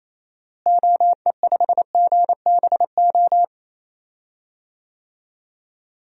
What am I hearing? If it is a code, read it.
Morse code, heard as a keyed tone, about 28 wpm, 700 Hz. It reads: OE5GBO
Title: DXCC-100 Call Signs - 6 Characters - Repeated 2x - 28wpm